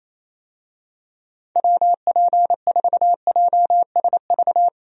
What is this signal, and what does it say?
Morse code, heard as a keyed tone, about 28 wpm, 700 Hz. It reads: WP4JSV
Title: USA Call Signs - 6 Characters - 28wpm